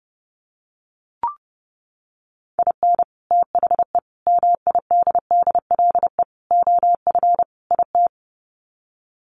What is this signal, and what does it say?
Morse code, heard as a keyed tone, about 30 wpm, 700 Hz. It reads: IN THE MIDDLE OF IT
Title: Sets of 5 Words Encore 30wpm